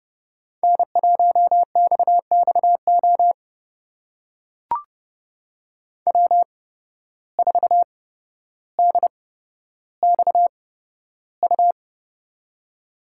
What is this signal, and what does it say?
Morse code, heard as a keyed tone, about 30 wpm, spaced as for 13 wpm, 700 Hz. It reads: N1XXO W4DXU
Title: USA Call Signs - 5 Characters - 8x Character Spacing - 30wpm